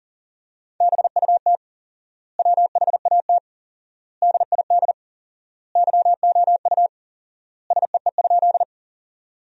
Morse code, heard as a keyed tone, about 40 wpm, 700 Hz. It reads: BUT WHAT DID YOU SEE?
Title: 4x Word-Spacing - Sentences from Top 100 Words - 40wpm